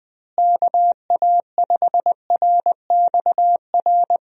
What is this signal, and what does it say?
Morse code, heard as a keyed tone, about 21 wpm, 700 Hz. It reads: KA5RXR
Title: USA Call Signs - 6 Characters - Repeated 3x - 20wpm